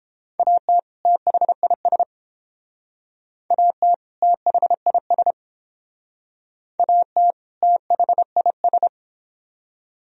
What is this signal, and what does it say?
Morse code, heard as a keyed tone, about 29 wpm, 700 Hz. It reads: AT THIS AT THIS AT THIS